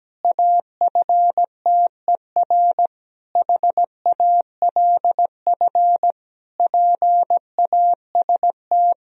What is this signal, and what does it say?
Morse code, heard as a keyed tone, about 17 wpm, 700 Hz. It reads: AFTER HALF PAST
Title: Sentences from Top 1000 Words - Review Part 3 - 17wpm